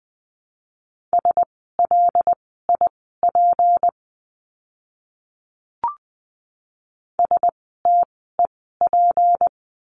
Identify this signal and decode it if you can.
Morse code, heard as a keyed tone, about 20 wpm, spaced as for 14 wpm, 700 Hz. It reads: SLIP STEP
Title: 153 Lesson 25f - Letter p - Words 20wpm